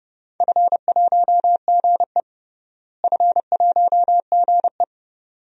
Morse code, heard as a keyed tone, about 30 wpm, 700 Hz. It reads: F1GE F1GE
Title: DXCC-100 Call Signs - 4 Characters - Repeated 2x - 30wpm